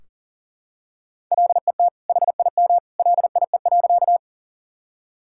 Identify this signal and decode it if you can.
Morse code, heard as a keyed tone, about 40 wpm, 700 Hz. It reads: LET HIM LIE.
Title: Sentences from Top 1600 Words 40wpm